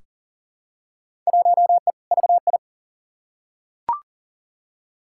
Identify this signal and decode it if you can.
Morse code, heard as a keyed tone, about 40 wpm, 700 Hz. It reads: 1E VI
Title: Field Day Exchanges Encore - 40wpm